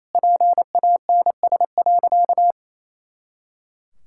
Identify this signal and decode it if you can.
Morse code, heard as a keyed tone, about 27 wpm, 700 Hz. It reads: PANS.
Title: Pangram Challenge v2 - 28wpm